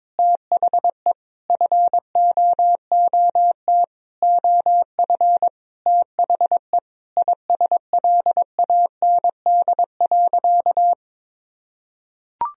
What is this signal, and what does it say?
Morse code, heard as a keyed tone, about 22 wpm, 700 Hz. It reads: THE FOOT OF THE ISLAND.
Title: Sentences from Top 1000 Words - Review Part 3 - 22wpm